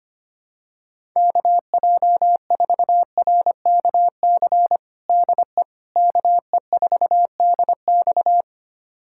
Morse code, heard as a keyed tone, about 25 wpm, 700 Hz. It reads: KJ4RKC DE KE4DX
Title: USA Callsign DE Callsign 25wpm